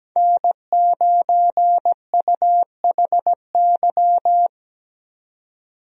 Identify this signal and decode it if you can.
Morse code, heard as a keyed tone, about 17 wpm, 700 Hz. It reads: N9UHY
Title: USA Call Signs - 5 Characters - 17wpm